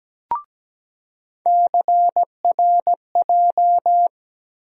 Morse code, heard as a keyed tone, about 17 wpm, 700 Hz. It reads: CRJ
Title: USA Call Sign Postfixes - 3 Letters 17wpm